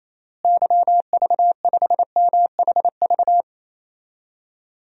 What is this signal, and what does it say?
Morse code, heard as a keyed tone, about 28 wpm, 700 Hz. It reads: YV5MHV
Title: DXCC-100 Call Signs - 6 Characters - Repeated 3x - 28wpm